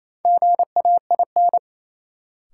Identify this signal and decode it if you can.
Morse code, heard as a keyed tone, about 28 wpm, 700 Hz. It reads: GAIN
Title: Sentences from Top 1000 Words - New Word Drill - 28wpm